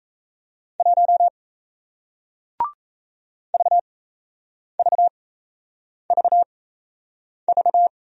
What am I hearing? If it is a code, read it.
Morse code, heard as a keyed tone, about 38 wpm, spaced as for 8 wpm, 700 Hz. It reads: 1 VVVV